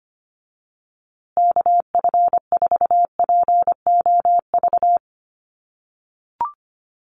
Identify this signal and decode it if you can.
Morse code, heard as a keyed tone, about 25 wpm, 700 Hz. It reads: KF4POV
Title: USA Call Signs Repeated 3x 25wpm